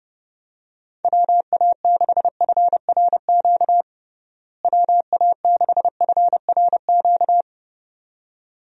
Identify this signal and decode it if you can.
Morse code, heard as a keyed tone, about 30 wpm, 700 Hz. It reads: WA6FRQ WA6FRQ
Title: USA Call Signs - 6 Characters - Repeated 2x - 30wpm